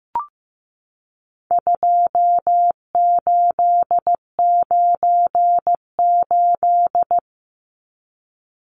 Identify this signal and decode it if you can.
Morse code, heard as a keyed tone, about 15 wpm, 700 Hz. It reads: 2898
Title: Numbers - 4-Digit 15wpm